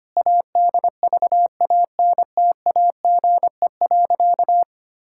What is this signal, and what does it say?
Morse code, heard as a keyed tone, about 25 wpm, 700 Hz. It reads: ADVANTAGE.
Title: Sentences from Top 1000 Words - Review Part 6 - 25wpm